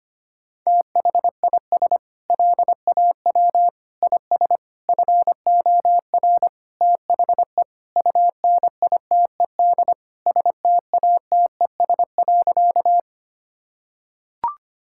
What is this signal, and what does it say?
Morse code, heard as a keyed tone, about 25 wpm, 700 Hz. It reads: THIS LAW IS FOR THE UNITED STATES.
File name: practice / 25wpm / Sentences with 4-Word Phrases - 25wpm